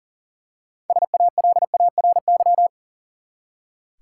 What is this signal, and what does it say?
Morse code, heard as a keyed tone, about 38 wpm, 700 Hz. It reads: SALARY
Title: Six Letter Words - 40wpm